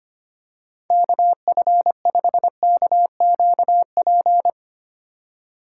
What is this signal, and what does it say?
Morse code, heard as a keyed tone, about 25 wpm, 700 Hz. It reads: KF5KQP
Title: USA Call Signs - 6 Characters - 2x Character Spacing - 25wpm